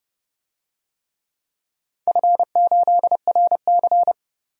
Morse code, heard as a keyed tone, about 30 wpm, 700 Hz. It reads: F8RC
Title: DXCC-100 Call Signs - 4 Characters - 30wpm